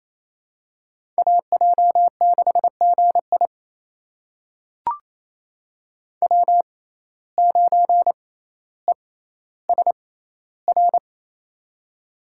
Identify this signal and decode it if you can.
Morse code, heard as a keyed tone, about 28 wpm, spaced as for 14 wpm, 700 Hz. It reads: AJ6GI W9ESR
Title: USA Call Signs - 5 Characters - 6x Character Spacing - 28wpm